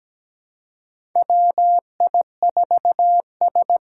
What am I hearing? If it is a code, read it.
Morse code, heard as a keyed tone, about 17 wpm, 700 Hz. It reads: WI4S
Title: USA Call Signs - 4 Characters - 4x Character Spacing - 17wpm